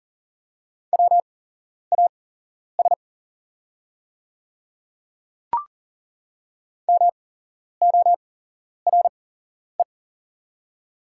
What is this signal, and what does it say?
Morse code, heard as a keyed tone, about 40 wpm, spaced as for 11 wpm, 700 Hz. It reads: WAS MORE
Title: 8x Character Spacing - Top 200 Words - 40wpm